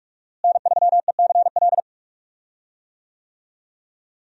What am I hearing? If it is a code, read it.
Morse code, heard as a keyed tone, about 41 wpm, 700 Hz. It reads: N3ECL
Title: USA Call Signs - 5 Characters - Repeated 3x - 45wpm